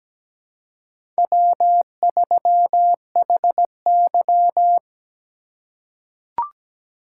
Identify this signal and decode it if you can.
Morse code, heard as a keyed tone, about 17 wpm, 700 Hz. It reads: W3HY
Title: USA Call Signs - 4 Characters - 2x Character Spacing - 17wpm